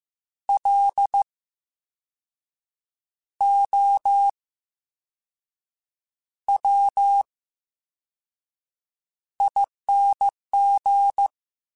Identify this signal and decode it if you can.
Morse code, heard as a keyed tone, about 15 wpm, spaced as for 5 wpm, 800 Hz. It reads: LOWING
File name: Special INGs-mp3s / Special INGs-mp3s / VerySlowToFaster Ing @13